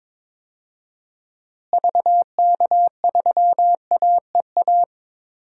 Morse code, heard as a keyed tone, about 22 wpm, 700 Hz. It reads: VK3AEA
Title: DXCC-100 Call Signs - 6 Characters - Repeated 2x - 22wpm